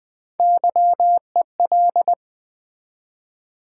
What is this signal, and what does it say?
Morse code, heard as a keyed tone, about 20 wpm, 700 Hz. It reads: YEL
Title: Build A Word - 1 Letter At A Time - 4-Letter Words - 20wpm